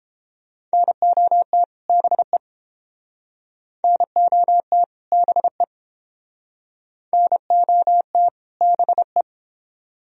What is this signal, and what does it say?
Morse code, heard as a keyed tone, about 30 wpm, 700 Hz. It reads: NOT BE NOT BE NOT BE